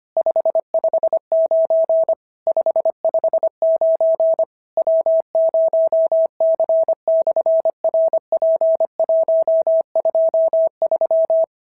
Morse code, heard as a keyed tone, about 25 wpm, 650 Hz. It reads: 559 559 W0C/RP123
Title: SOTA QSO Practice 25wpm